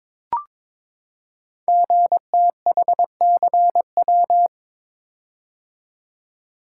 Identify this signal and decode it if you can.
Morse code, heard as a keyed tone, about 22 wpm, 700 Hz. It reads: GTHCW
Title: Code Groups - 5 Letters - 22wpm